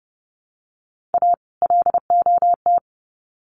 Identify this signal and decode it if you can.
Morse code, heard as a keyed tone, about 30 wpm, 700 Hz. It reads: A LOT
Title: Sets of 2 Words 30wpm